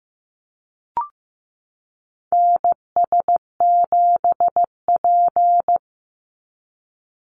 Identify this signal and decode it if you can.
Morse code, heard as a keyed tone, about 15 wpm, 700 Hz. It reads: NS7P
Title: NA SOTA Activator Callsigns 15wpm